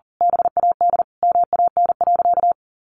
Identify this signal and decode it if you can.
Morse code, heard as a keyed tone, about 40 wpm, 700 Hz. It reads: BAD MAN.
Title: Sentences from Top 700 Words 40wpm